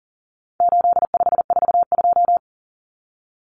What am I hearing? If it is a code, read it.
Morse code, heard as a keyed tone, about 38 wpm, 700 Hz. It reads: 8542
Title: Common Cut Numbers - 4-Digit 40wpm